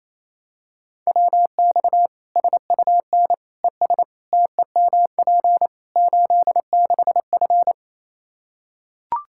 Morse code, heard as a keyed tone, about 28 wpm, 700 Hz. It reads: WX SUN ES TEMP 86F
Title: K1USN SST QSO Practice 28wpm